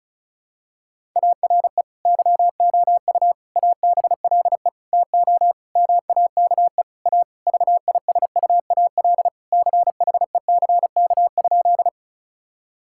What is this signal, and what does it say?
Morse code, heard as a keyed tone, about 36 wpm, 700 Hz. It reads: ARE YOU ABLE TO MAKE A VISUAL CHECK?
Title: Sentences from Top 1000 Words - Review Part 6 - 35wpm